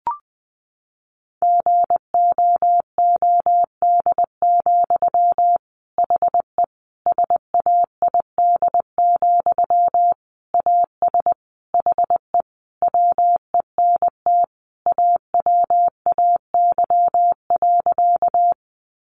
Morse code, heard as a keyed tone, about 20 wpm, 700 Hz. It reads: GOOD, HE SAID, AS HE WENT AWAY.
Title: Sentences from Top 600 Words 20wpm